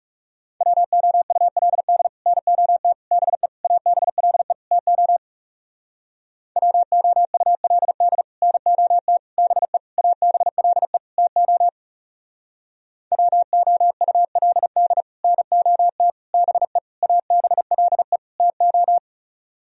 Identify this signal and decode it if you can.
Morse code, heard as a keyed tone, about 37 wpm, 700 Hz. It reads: WOULD NOT BE ABLE TO WOULD NOT BE ABLE TO WOULD NOT BE ABLE TO
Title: Speed-Racing - Sets of 5 Words 30wpm